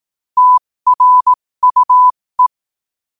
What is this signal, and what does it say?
Morse code, heard as a keyed tone, about 18 wpm, spaced as for 15 wpm, 1000 Hz. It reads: TRUE
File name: qotdfiles / 15wpm